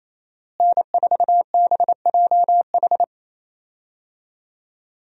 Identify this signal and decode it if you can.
Morse code, heard as a keyed tone, about 28 wpm, 700 Hz. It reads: N4BJH
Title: USA Call Signs - 5 Characters - Repeated 3x - 28wpm